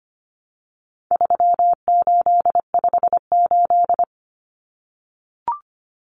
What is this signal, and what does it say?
Morse code, heard as a keyed tone, about 25 wpm, 700 Hz. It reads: 3858
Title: Common Cut Numbers - 4-Digit 25wpm